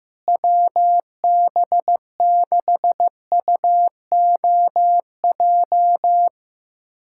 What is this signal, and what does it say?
Morse code, heard as a keyed tone, about 15 wpm, 700 Hz. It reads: WB6UOJ
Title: USA Callsign DE Callsign - 6 Characters - 15wpm